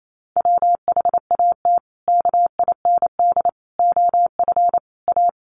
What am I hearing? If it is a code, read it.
Morse code, heard as a keyed tone, about 28 wpm, 700 Hz. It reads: WHAT KIND OF A
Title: Sentences from Top 400 Words 28wpm